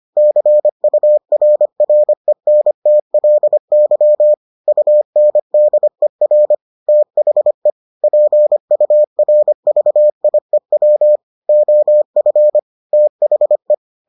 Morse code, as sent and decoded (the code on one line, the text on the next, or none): -.-. ..- .-. .-. -. - .-.. -.-- / ..- -. -.. .-. / - .... / .--. ..- .-. ...- .. .-- / --- ..-. / - .... .
CURRENTLY UNDER THE PURVIEW OF THE